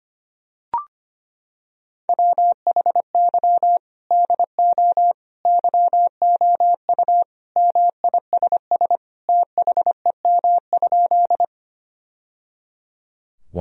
Morse code, heard as a keyed tone, about 25 wpm, 700 Hz. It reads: WHY DO YOU MISS THEM?
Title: Sentences from Top 1000 Words - Review Part 3 - 25wpm